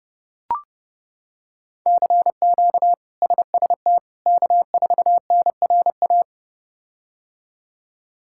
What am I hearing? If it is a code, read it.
Morse code, heard as a keyed tone, about 31 wpm, 700 Hz. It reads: CQ SST K4NRA
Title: K1USN SST QSO Practice - 30wpm